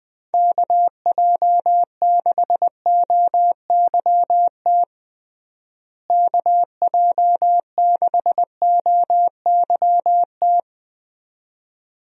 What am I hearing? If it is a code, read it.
Morse code, heard as a keyed tone, about 20 wpm, 700 Hz. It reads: KJ6OYT KJ6OYT
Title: USA Call Signs - 6 Characters - Repeated 2x - 20wpm